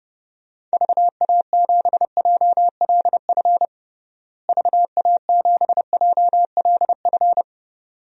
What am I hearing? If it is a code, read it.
Morse code, heard as a keyed tone, about 30 wpm, 700 Hz. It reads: VA7JLF VA7JLF
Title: DXCC-100 Call Signs - 6 Characters - Repeated 2x - 30wpm